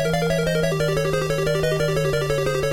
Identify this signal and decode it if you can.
There is no code.